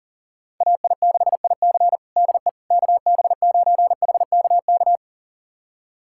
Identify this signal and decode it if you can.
Morse code, heard as a keyed tone, about 37 wpm, 700 Hz. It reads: AI6IC DE KB9HKK
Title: USA Callsign DE Callsign 40wpm